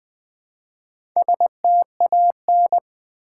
Morse code, heard as a keyed tone, about 20 wpm, 700 Hz. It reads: STAN
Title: Build A Word - 1 Letter At A Time - 5-Letter Words - 20wpm